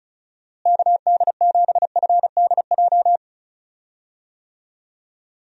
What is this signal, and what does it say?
Morse code, heard as a keyed tone, about 36 wpm, 700 Hz. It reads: KD7FDJ
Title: USA Call Signs - 6 Characters - Repeated 3x - 35wpm